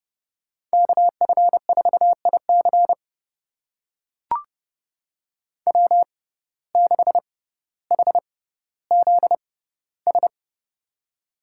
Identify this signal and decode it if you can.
Morse code, heard as a keyed tone, about 31 wpm, spaced as for 16 wpm, 700 Hz. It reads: KF4IC W6HZS
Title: USA Call Signs - 5 Characters - 6x Character Spacing - 30wpm